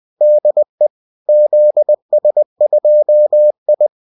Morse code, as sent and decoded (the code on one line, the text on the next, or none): -.. / --.. ... ..--- ..
DE ZS2I